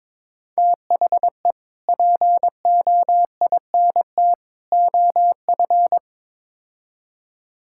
Morse code, heard as a keyed tone, about 22 wpm, 700 Hz. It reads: THE POINT OF
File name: practice / 22wpm / Build A Phrase - 5-Word Phrases - 22wpm